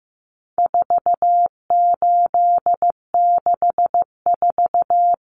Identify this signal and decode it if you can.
Morse code, heard as a keyed tone, about 15 wpm, 700 Hz. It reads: 4864
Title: Numbers - 4-Digit 15wpm